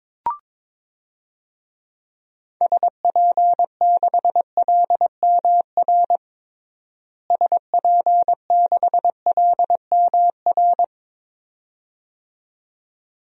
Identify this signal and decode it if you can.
Morse code, heard as a keyed tone, about 22 wpm, 700 Hz. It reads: SP6LMR SP6LMR
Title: DXCC-100 Call Signs - 6 Characters - Repeated 2x - 22wpm